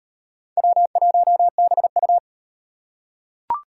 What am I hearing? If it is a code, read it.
Morse code, heard as a keyed tone, about 33 wpm, 700 Hz. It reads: W1BU